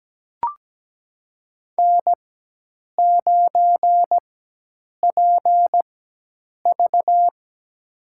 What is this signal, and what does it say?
Morse code, heard as a keyed tone, about 17 wpm, spaced as for 8 wpm, 700 Hz. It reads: N9PV
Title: USA Call Signs - 4x Character Spacing - 17wpm